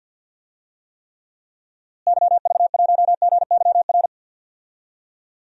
Morse code, heard as a keyed tone, about 49 wpm, 700 Hz. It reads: YV1GYR
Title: DXCC-100 Call Signs - 6 Characters - Repeated 2x - 50wpm